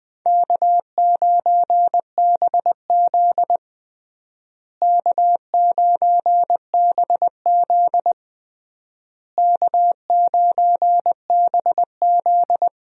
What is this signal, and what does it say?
Morse code, heard as a keyed tone, about 20 wpm, 700 Hz. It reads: K9BZ K9BZ K9BZ
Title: USA Call Signs - 4 Characters - Repeated 3x - 20wpm